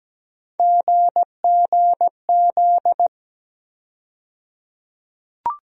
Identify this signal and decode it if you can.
Morse code, heard as a keyed tone, about 17 wpm, 700 Hz. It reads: GGZ
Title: USA Call Sign Postfixes - 3 Letters 17wpm